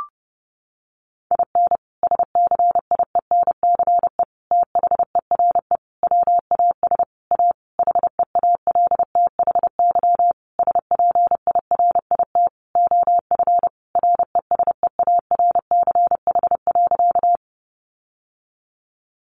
Morse code, heard as a keyed tone, about 30 wpm, 700 Hz. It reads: IN SCIENCE THERE WAS A HEALTHY SPIRIT OF RESEARCH.
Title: Sentences from Top 1400 Words 30wpm